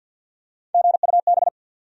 Morse code, heard as a keyed tone, about 52 wpm, 700 Hz. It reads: GUB
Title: USA Call Sign Postfixes - 3 Letters 50wpm